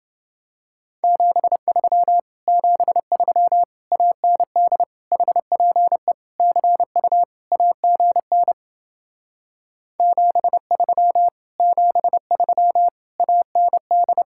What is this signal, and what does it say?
Morse code, heard as a keyed tone, about 30 wpm, 700 Hz. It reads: 73 73 AND HPE CU AGN 73 73 AND